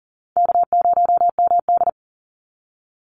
Morse code, heard as a keyed tone, about 40 wpm, 700 Hz. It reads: K0MD
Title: Call Signs - 40wpm